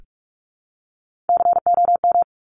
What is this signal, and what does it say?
Morse code, heard as a keyed tone, about 46 wpm, 700 Hz. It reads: COM
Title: Top Three Letter Combinations in Words 45wpm